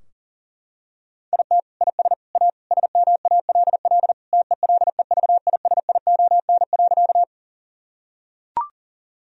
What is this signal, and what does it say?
Morse code, heard as a keyed tone, about 37 wpm, 700 Hz. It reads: IT IS A SMALL TELEVISION.
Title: Sentences from Top 1000 Words - Review Part 4 - 40wpm